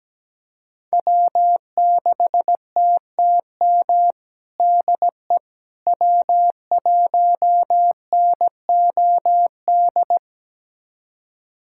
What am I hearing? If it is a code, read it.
Morse code, heard as a keyed tone, about 17 wpm, 700 Hz. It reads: W6TTM DE W1NOD
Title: USA Callsign DE Callsign - 5 Characters - 17wpm